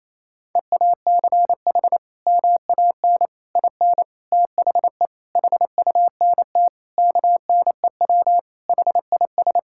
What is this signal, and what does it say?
Morse code, heard as a keyed tone, about 28 wpm, 700 Hz. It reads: EACH MAN IN THE HUNT KNEW HIS
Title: Sentences from Top 1600 Words 28wpm